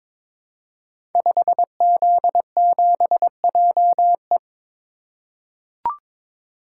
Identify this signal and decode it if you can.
Morse code, heard as a keyed tone, about 22 wpm, 700 Hz. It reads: 5Z7JE
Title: Code Groups - 5 Letter:Numbers - 22wpm